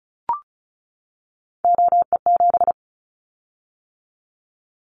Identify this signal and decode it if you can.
Morse code, heard as a keyed tone, about 34 wpm, 700 Hz. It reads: OE7
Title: Callsign Prefixes for Top 25 DXCC - 35wpm